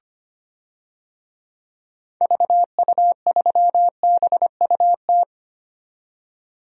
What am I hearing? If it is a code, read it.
Morse code, heard as a keyed tone, about 25 wpm, 700 Hz. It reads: VU3BUT
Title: DXCC-100 Call Signs - 6 Characters - 25wpm